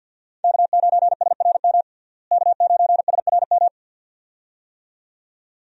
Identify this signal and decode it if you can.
Morse code, heard as a keyed tone, about 51 wpm, 700 Hz. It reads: K9SRM K9SRM
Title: USA Call Signs - 5 Characters - Repeated 2x - 50wpm